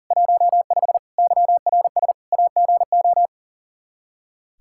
Morse code, heard as a keyed tone, about 37 wpm, 700 Hz. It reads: 15 YRS AGO
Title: Practice QSO - Exchange 3 - More Info - 40wpm